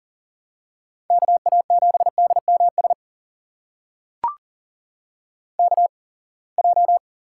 Morse code, heard as a keyed tone, about 40 wpm, spaced as for 28 wpm, 700 Hz. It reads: KA7DMS KJ
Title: USA Call Signs - 6 Characters - 8x Character Spacing - 40wpm